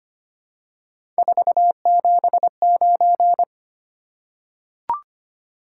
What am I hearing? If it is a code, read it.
Morse code, heard as a keyed tone, about 25 wpm, 700 Hz. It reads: 479
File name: Three Letter-Number Combinations - 25wpm